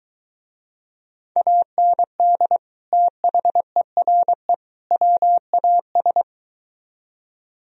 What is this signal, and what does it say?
Morse code, heard as a keyed tone, about 23 wpm, 700 Hz. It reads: AND THERE WAS